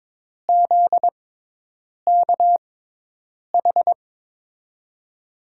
Morse code, heard as a keyed tone, about 22 wpm, spaced as for 8 wpm, 700 Hz. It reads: ZKH
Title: USA Call Signs - 5 Characters - 6x Character Spacing - 22wpm